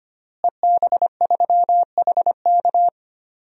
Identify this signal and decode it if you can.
Morse code, heard as a keyed tone, about 25 wpm, 700 Hz. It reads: EB3HK
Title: DXCC-100 Call Signs - 5 Characters - Repeated 2x - 25wpm